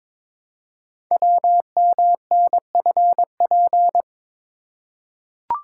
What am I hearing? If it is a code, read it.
Morse code, heard as a keyed tone, about 22 wpm, 700 Hz. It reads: WMNFP
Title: Code Groups - 5 Letters - 22wpm